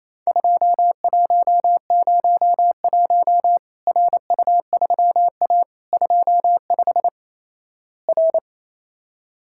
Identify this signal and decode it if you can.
Morse code, heard as a keyed tone, about 28 wpm, 700 Hz. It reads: 2101 RU3A 25 R
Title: Worked All Europe DX Contest - QTC Exchange - 28wpm